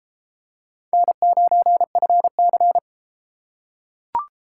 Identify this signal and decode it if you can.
Morse code, heard as a keyed tone, about 31 wpm, 700 Hz. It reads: N9FC